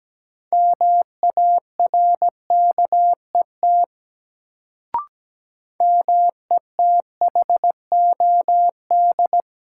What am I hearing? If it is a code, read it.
Morse code, heard as a keyed tone, about 17 wpm, 700 Hz. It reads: MARKET METHOD
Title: Sentences from Top 600 Words - New Word Drill - 17wpm